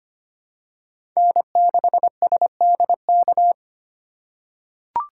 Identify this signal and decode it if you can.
Morse code, heard as a keyed tone, about 25 wpm, 700 Hz. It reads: N6SDK
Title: USA Call Signs - 5 Characters - 4x Character Spacing - 25wpm